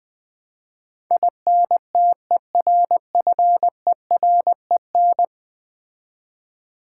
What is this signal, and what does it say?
Morse code, heard as a keyed tone, about 21 wpm, 700 Hz. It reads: INTERFEREN
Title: Build A Word - 5 Letters At A Time - 12-Letter Words - 20wpm